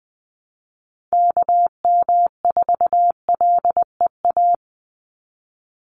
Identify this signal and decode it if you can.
Morse code, heard as a keyed tone, about 20 wpm, 700 Hz. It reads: KM4LEA
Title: USA Call Signs 20wpm